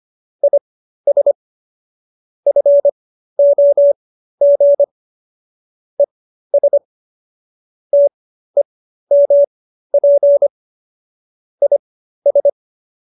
Morse code, heard as a keyed tone, about 25 wpm, spaced as for 13 wpm, 570 Hz. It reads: IS FOG ES TEMP IS